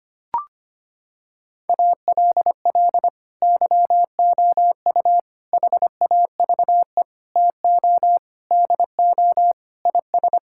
Morse code, heard as a keyed tone, about 25 wpm, 700 Hz. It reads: ALL YOU HAVE TO DO IS
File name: practice / 25wpm / Sentences with 9-Word Phrases - 25wpm